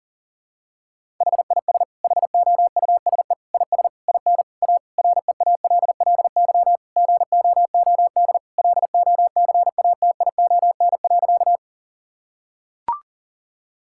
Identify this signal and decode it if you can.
Morse code, heard as a keyed tone, about 40 wpm, 700 Hz. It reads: HIS HOUSE IS IN A REALLY GOOD LOCATION.
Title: Sentences from Top 1000 Words - Review Part 6 - 40wpm